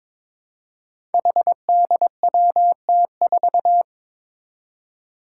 Morse code, heard as a keyed tone, about 22 wpm, 700 Hz. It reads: HDWT4
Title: Five Challenging Letter-Number Combinations 22wpm